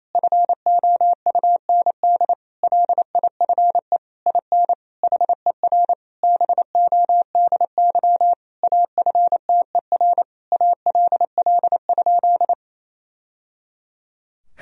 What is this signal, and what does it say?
Morse code, heard as a keyed tone, about 28 wpm, 700 Hz. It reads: FOUND LIFE IN HER BODY AFTER ALL?
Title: Sentences from Top 1000 Words - Review Part 2 - 28wpm